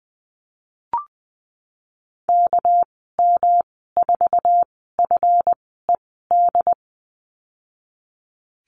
Morse code, heard as a keyed tone, about 20 wpm, spaced as for 14 wpm, 700 Hz. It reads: KM4FED
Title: 231 Lesson 37g - Letter x - Call Signs 20wpm